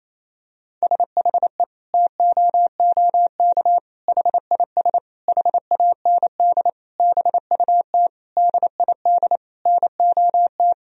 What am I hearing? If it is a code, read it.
Morse code, heard as a keyed tone, about 28 wpm, 700 Hz. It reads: SHE TOOK HIS HAND BUT DID NOT
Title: Sentences from Top 1000 Words - Review Part 1 - 28wpm